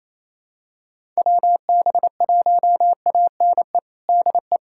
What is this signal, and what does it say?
Morse code, heard as a keyed tone, about 27 wpm, 700 Hz. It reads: WB1ANE DE
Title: USA Callsign DE Callsign - 6 Characters - 28wpm